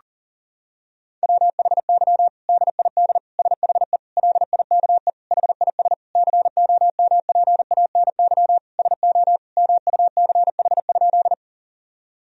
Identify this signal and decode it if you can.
Morse code, heard as a keyed tone, about 37 wpm, 700 Hz. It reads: WHY DID SHE LIKE HIS COMPANY SO MUCH?
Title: Sentences from Top 1000 Words - Review Part 3 - 40wpm